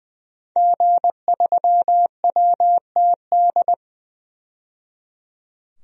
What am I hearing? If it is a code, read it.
Morse code, heard as a keyed tone, about 20 wpm, 700 Hz. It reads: G3WTD
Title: DXCC-100 Call Signs - 5 Characters - 20wpm